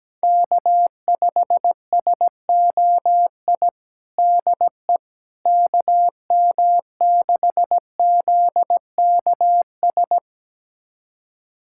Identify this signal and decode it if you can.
Morse code, heard as a keyed tone, about 17 wpm, 700 Hz. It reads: K5SOI DE KM6ZKS
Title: USA Callsign DE Callsign 17wpm